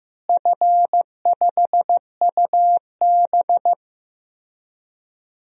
Morse code, heard as a keyed tone, about 16 wpm, 700 Hz. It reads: F5UB
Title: DXCC-100 Call Signs - 4 Characters - Repeated 3x - 15wpm